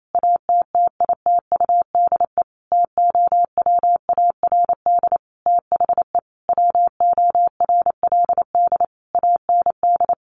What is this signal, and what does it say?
Morse code, heard as a keyed tone, about 28 wpm, 700 Hz. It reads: ATTITUDE TOWARD THE WORLD AND
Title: Sentences from Top 1300 Words 28wpm